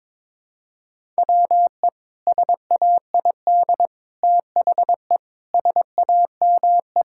Morse code, heard as a keyed tone, about 21 wpm, 700 Hz. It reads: WE SAID THE SAME
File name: practice / 22wpm / Sentences with 3-Word Phrases - 22wpm